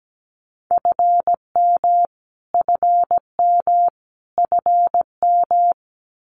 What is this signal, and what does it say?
Morse code, heard as a keyed tone, about 17 wpm, 700 Hz. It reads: FM FM FM
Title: Abbreviations Repeated 3x - 17wpm